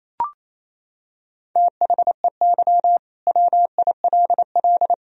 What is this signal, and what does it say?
Morse code, heard as a keyed tone, about 28 wpm, 700 Hz. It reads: THEY WILL
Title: Sentences from Top 1000 Words - Review Part 4 - 28wpm